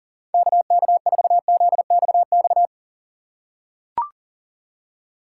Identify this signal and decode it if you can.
Morse code, heard as a keyed tone, about 40 wpm, 700 Hz. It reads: KK4ZXX
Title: USA Call Signs - 4x Character Spacing - 40wpm